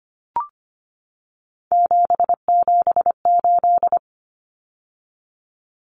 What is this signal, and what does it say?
Morse code, heard as a keyed tone, about 25 wpm, 690 Hz. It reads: 778
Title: Numbers - 3-Digit 25wpm